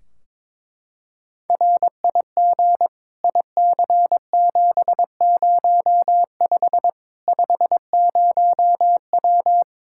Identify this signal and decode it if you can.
Morse code, heard as a keyed tone, about 21 wpm, 700 Hz. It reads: RIG IC705 50W
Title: Practice QSO - Exchange 2 - Info - 22wpm